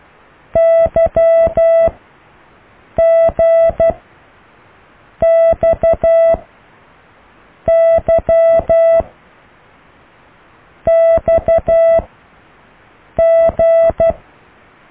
Morse code, heard as a keyed tone, about 12 wpm, 660 Hz. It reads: Y G X Y X G